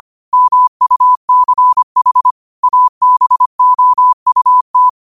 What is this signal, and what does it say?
Morse code, heard as a keyed tone, about 25 wpm, 1000 Hz. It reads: MUCH ABOUT